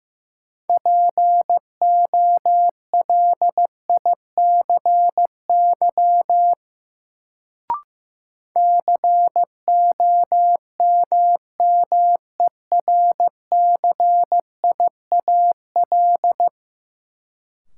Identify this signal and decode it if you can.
Morse code, heard as a keyed tone, about 15 wpm, 700 Hz. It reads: POLICY COMMERCIAL
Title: Sentences from Top 1000 Words - Review Part 5 - New Word Drill - 15wpm